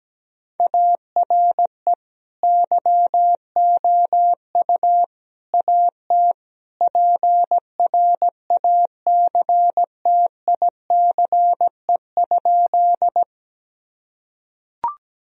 Morse code, heard as a keyed tone, about 17 wpm, 700 Hz. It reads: ARE YOU AT PRACTICE?
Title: Sentences from Top 1000 Words - Review Part 4 - 17wpm